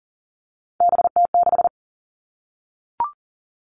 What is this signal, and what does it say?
Morse code, heard as a keyed tone, about 40 wpm, 700 Hz. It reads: BT6
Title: All Cut Numbers - 3-Digit 40wpm